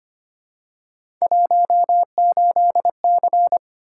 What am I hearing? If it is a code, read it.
Morse code, heard as a keyed tone, about 25 wpm, 700 Hz. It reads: 18C
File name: Three Letter-Number Combinations - 25wpm